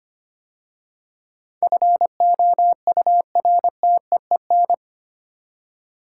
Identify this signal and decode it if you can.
Morse code, heard as a keyed tone, about 25 wpm, 700 Hz. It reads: FOURTEEN
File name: 8 Letter Words - First 3 Letters Given - 25wpm